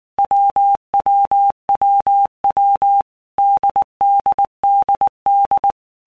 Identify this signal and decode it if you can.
Morse code, heard as a keyed tone, about 19 wpm, 800 Hz. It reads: WWWW DDDD
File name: Basic Letter Groups By Blocks / Groups@18 / AWUNDG@18